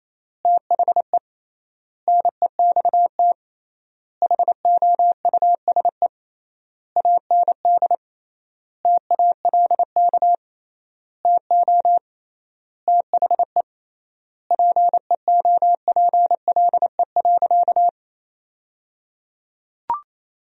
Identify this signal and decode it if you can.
Morse code, heard as a keyed tone, about 28 wpm, 700 Hz. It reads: THE NEXT HOUSE AND TALK TO THE PEOPLE.
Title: 3x Word-Spacing - Sentences from Top 400 Words - 28wpm